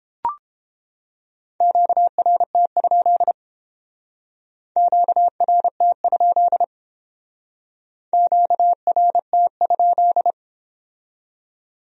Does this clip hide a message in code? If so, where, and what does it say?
Yes: Morse code from 1.6 to 10.3 s, heard as a keyed tone, about 28 wpm, 700 Hz. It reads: QRT? QRT? QRT?